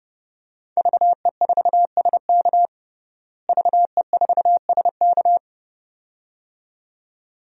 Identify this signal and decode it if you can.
Morse code, heard as a keyed tone, about 30 wpm, 700 Hz. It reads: VE4SK VE4SK
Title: DXCC-100 Call Signs - 5 Characters - Repeated 2x - 30wpm